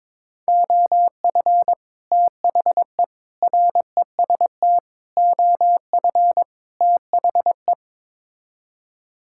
Morse code, heard as a keyed tone, about 21 wpm, 700 Hz. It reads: OF THE REST OF THE
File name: practice / 22wpm / Build A Phrase - 5-Word Phrases - 22wpm